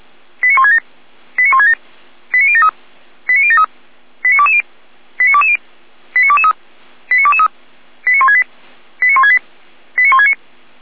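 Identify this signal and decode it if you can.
Five-tone selective call (ZVEI), heard as quick runs of five tones, tones of about 70 ms. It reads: 89167 89167 89073 89073 89220 89220 89223 89223 89168 89168 89169
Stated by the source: Five-tone ZVEI selective call received on SDR.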